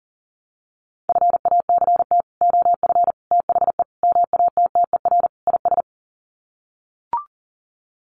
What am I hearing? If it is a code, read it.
Morse code, heard as a keyed tone, about 40 wpm, 700 Hz. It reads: FACT OF THE MATTER IS
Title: Sets of 5 Words 40wpm